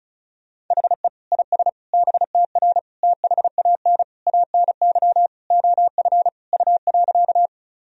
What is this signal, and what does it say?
Morse code, heard as a keyed tone, about 36 wpm, 700 Hz. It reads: HE IS BTR THAN ANY OF U.
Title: Sentences from Top 400 Words and Abbreviations 35wpm